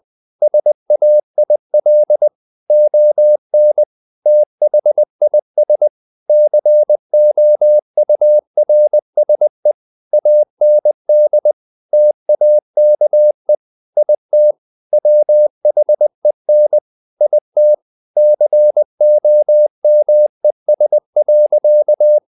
Morse code, heard as a keyed tone, about 20 wpm, 600 Hz. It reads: SAIL ON THIS COURSE AND TAKE IT WHEN IT COMES.
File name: the old man and the sea part 2 20wpm